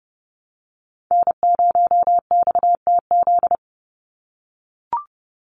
Morse code, heard as a keyed tone, about 30 wpm, 700 Hz. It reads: N0XTZ
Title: USA Call Signs 30wpm